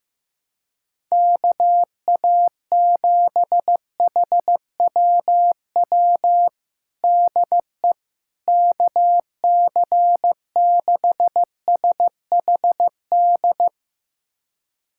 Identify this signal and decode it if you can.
Morse code, heard as a keyed tone, about 15 wpm, 700 Hz. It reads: KA7HWW DE KC6SHD
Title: USA Callsign DE Callsign 15wpm